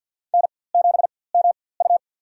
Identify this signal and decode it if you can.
Morse code, heard as a keyed tone, about 52 wpm, spaced as for 24 wpm, 700 Hz. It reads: N7MU
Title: USA Call Signs - 4x Character Spacing - 50wpm